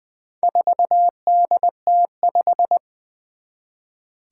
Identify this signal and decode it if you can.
Morse code, heard as a keyed tone, about 20 wpm, 700 Hz. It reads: 4DT5
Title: Four Challenging Letter-Number Combinations 20wpm